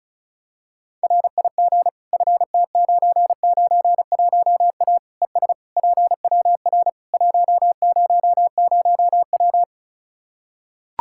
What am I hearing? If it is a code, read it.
Morse code, heard as a keyed tone, about 36 wpm, 700 Hz. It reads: RIG FT991A ES PWR 100W
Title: Practice QSO - Exchange 2 - Info - 35wpm